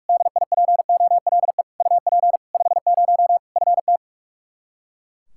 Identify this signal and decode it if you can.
Morse code, heard as a keyed tone, about 41 wpm, 700 Hz. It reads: DIPOLE UP 50 FT
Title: Practice QSO - Exchange 2 - Info - 45wpm